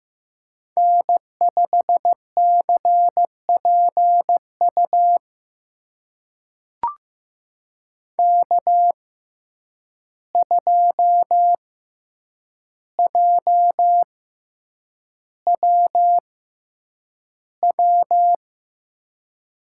Morse code, heard as a keyed tone, about 15 wpm, spaced as for 8 wpm, 700 Hz. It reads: N5CPU K2JWW
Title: USA Call Signs - 5 Characters - 6x Character Spacing - 15wpm